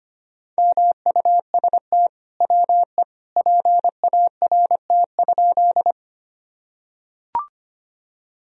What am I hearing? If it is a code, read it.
Morse code, heard as a keyed tone, about 25 wpm, 700 Hz. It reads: MUST WE PART?
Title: Sentences from Top 1000 Words - Review Part 1 - 25wpm